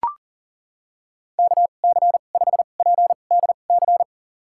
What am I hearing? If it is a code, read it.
Morse code, heard as a keyed tone, about 39 wpm, spaced as for 28 wpm, 700 Hz. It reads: KC5PDC
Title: USA Call Signs - 2x Character Spacing - 40wpm